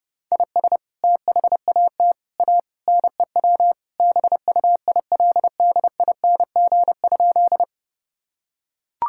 Morse code, heard as a keyed tone, about 30 wpm, 700 Hz. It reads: IS THAT A NEW BUILDING?
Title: Sentences from Top 1000 Words - Review Part 5 - 30wpm